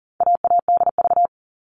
Morse code, heard as a keyed tone, about 40 wpm, 700 Hz. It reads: AADV
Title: All Cut Numbers - 4-Digit 40wpm